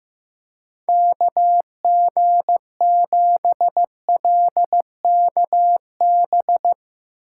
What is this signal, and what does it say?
Morse code, heard as a keyed tone, about 16 wpm, 700 Hz. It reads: KG7LKB
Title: Speed Racing - USA 2x3 Call Signs 15wpm